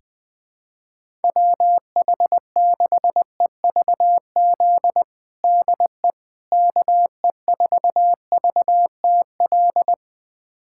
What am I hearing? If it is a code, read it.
Morse code, heard as a keyed tone, about 20 wpm, 700 Hz. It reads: WH6EVZ DE KE4VTL
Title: USA Callsign DE Callsign - 6 Characters - 20wpm